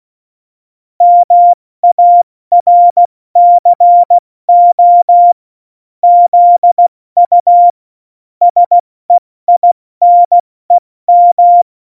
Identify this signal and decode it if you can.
Morse code, heard as a keyed tone, about 16 wpm, 700 Hz. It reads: MARCO ZU SEINEM
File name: Morsekurs / Witze / Witz 03. Oktober 17, Tempo 70 F80